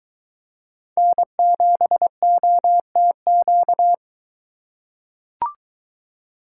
Morse code, heard as a keyed tone, about 23 wpm, 700 Hz. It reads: N7OTQ